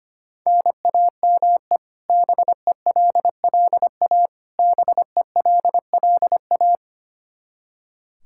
Morse code, heard as a keyed tone, about 25 wpm, 700 Hz. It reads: NAME BELLA BELLA
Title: Practice QSO - Exchange 1 - Beginning - 25wpm